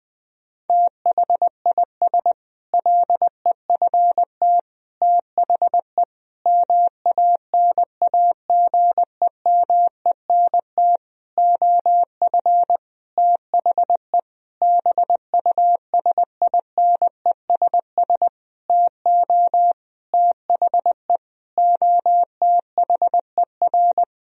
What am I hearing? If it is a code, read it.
Morse code, heard as a keyed tone, about 20 wpm, 700 Hz. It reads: THIS LEFT THE MANAGEMENT OF THE BUSINESS TO THE OTHER
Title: Sentences from Top 1000 Words - Review Part 4 - 20wpm